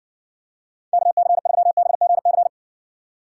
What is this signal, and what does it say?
Morse code, heard as a keyed tone, about 64 wpm, 700 Hz. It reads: KC3BGC